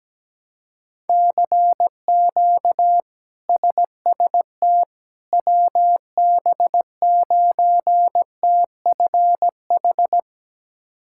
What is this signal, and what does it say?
Morse code, heard as a keyed tone, about 17 wpm, 700 Hz. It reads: CQ SST WB9TFH
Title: K1USN SST QSO Practice - 17wpm